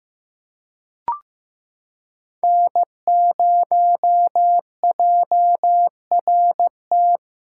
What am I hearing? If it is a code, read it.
Morse code, heard as a keyed tone, about 16 wpm, 700 Hz. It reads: N0JRT
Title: USA Callsign DE Callsign - 5 Characters - 15wpm